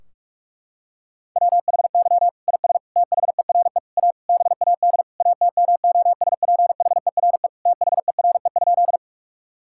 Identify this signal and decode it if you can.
Morse code, heard as a keyed tone, about 46 wpm, 700 Hz. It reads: WHY IS THERE A BAD ATMOSPHERE THERE?
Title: Sentences from Top 1600 Words 45wpm